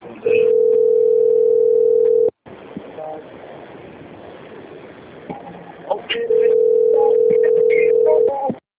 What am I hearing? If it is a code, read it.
Ringback tone (North America), 440 + 480 Hz, with speech.
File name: out- - - - - .31569